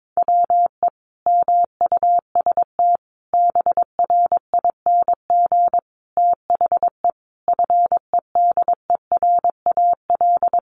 Morse code, heard as a keyed tone, about 22 wpm, 700 Hz. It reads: WE MUST BRING THE FEDERAL
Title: Sentences from Top 1200 Words 22wpm